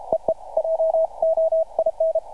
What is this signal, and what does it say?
Morse code, heard as a keyed tone, about 32 wpm, 650 Hz. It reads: EE JOIN
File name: archived / CRSnet